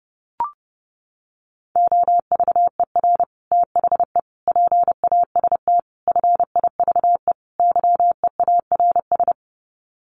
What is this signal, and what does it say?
Morse code, heard as a keyed tone, about 30 wpm, 700 Hz. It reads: OVER THE PAST FIVE YEARS
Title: Sets of 5 Words Encore 30wpm